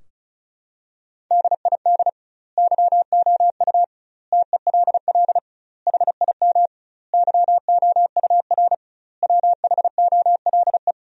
Morse code, heard as a keyed tone, about 34 wpm, 700 Hz. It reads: DID YOU TELL HIM YOUR WHOLE
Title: 2x Word-Spacing - Sentences from Top 400 Words - 35wpm